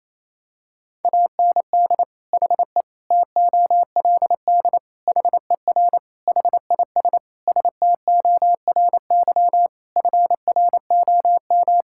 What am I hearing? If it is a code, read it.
Morse code, heard as a keyed tone, about 27 wpm, 700 Hz. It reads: AND HE TOLD HER HIS STORY FROM
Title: Sentences from Top 1000 Words - Review Part 5 - 28wpm